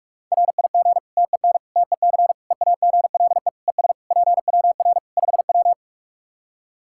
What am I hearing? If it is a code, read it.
Morse code, heard as a keyed tone, about 46 wpm, 700 Hz. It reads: RIG TEN TEC EAGLE ES PWR 5W
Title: Practice QSO - Exchange 2 - Info - 45wpm